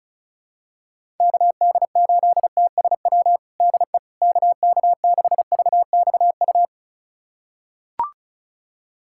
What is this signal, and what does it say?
Morse code, heard as a keyed tone, about 36 wpm, 700 Hz. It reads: KD8TSW DE KK6VXU
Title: USA Callsign DE Callsign - 6 Characters - 35wpm